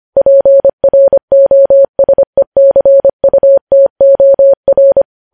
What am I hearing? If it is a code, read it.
Morse code, heard as a keyed tone, about 25 wpm, 550 Hz. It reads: PROSECUTOR
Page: news headlines in morse code at 25 wpm